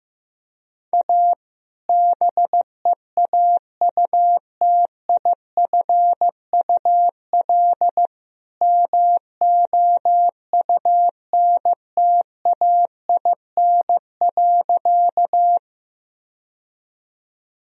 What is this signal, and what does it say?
Morse code, heard as a keyed tone, about 15 wpm, 700 Hz. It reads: A BEAUTIFUL MOUNTAIN.
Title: Sentences from Top 1000 Words - Review Part 1 - 15wpm